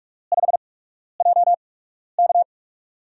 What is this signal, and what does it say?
Morse code, heard as a keyed tone, about 42 wpm, spaced as for 12 wpm, 700 Hz. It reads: 5JK
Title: USA Call Signs - 6 Characters - 8x Character Spacing - 45wpm